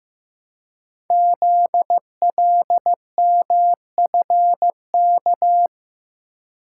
Morse code, heard as a keyed tone, about 15 wpm, 700 Hz. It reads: ZLMFK
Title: Code Groups - 5 Letters - 15wpm